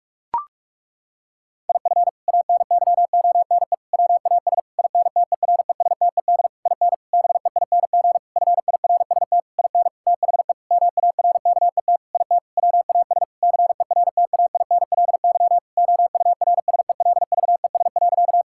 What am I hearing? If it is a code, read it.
Morse code, heard as a keyed tone, about 46 wpm, 700 Hz. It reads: IF ANYONE WAS INTERESTED IN BEING FIRST IN THE MARKET IT WAS CERTAINLY OURSELVES.
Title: Sentences from Top 1600 Words 45wpm